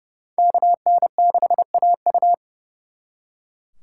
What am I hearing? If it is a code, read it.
Morse code, heard as a keyed tone, about 38 wpm, 700 Hz. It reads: KN6AU